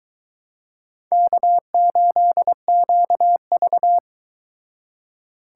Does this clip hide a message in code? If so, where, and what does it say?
Yes: Morse code from 1.1 to 4.0 s, heard as a keyed tone, about 22 wpm, 700 Hz. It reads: K8QV